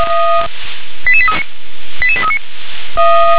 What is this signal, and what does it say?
Five-tone selective call (ZVEI), heard as quick runs of five tones, tones of about 70 ms. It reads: 88839 88839